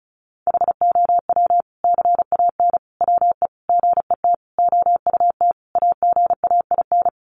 Morse code, heard as a keyed tone, about 34 wpm, 700 Hz. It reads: HOW CAN WE GET OUT AGAIN
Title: Sentences from Top 1200 Words 35wpm